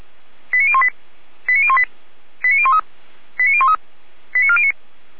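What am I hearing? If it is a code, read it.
Five-tone selective call (ZVEI), heard as quick runs of five tones, tones of about 70 ms. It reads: 89018 89018 89013 89013 89409
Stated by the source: ZVEI five-tone selective call received by SDR, address 89560.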